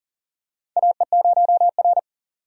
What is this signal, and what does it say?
Morse code, heard as a keyed tone, about 40 wpm, 700 Hz. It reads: AE0R
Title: USA Call Signs - 4 Characters - 40wpm